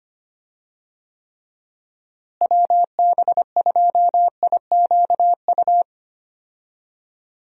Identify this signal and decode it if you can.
Morse code, heard as a keyed tone, about 25 wpm, 700 Hz. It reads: WB2IQU
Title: USA Call Signs - 6 Characters - Repeated 3x - 25wpm